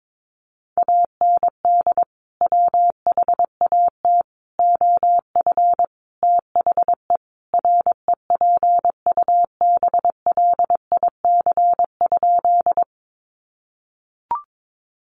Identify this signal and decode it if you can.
Morse code, heard as a keyed tone, about 22 wpm, 700 Hz. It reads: AND WHAT OF THE REPUBLIC?
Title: Sentences from Top 1500 Words 22wpm